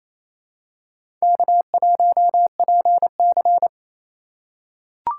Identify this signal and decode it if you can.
Morse code, heard as a keyed tone, about 28 wpm, 700 Hz. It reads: K1PC
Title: USA Call Signs - 4 Characters - 2x Character Spacing - 28wpm